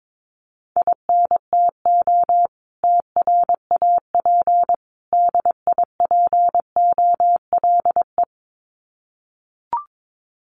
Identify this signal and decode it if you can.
Morse code, heard as a keyed tone, about 22 wpm, 700 Hz. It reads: INTO TRAP DIPOLE
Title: Common QSO Phrases 22wpm